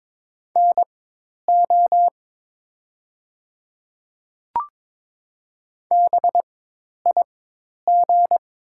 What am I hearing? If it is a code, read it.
Morse code, heard as a keyed tone, about 22 wpm, spaced as for 10 wpm, 700 Hz. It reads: NO BIG
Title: 4x Character Spacing - Top 200 Words - 22wpm